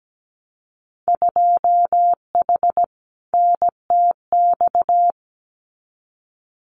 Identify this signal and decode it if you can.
Morse code, heard as a keyed tone, about 17 wpm, 700 Hz. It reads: 2H NTX
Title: Winter Field Day Exchanges Encore 17wpm